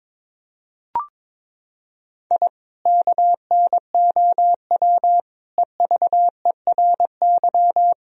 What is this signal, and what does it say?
Morse code, heard as a keyed tone, about 22 wpm, 700 Hz. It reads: I KNOW EVERY
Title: Sentences from Top 1000 Words - Review Part 3 - 22wpm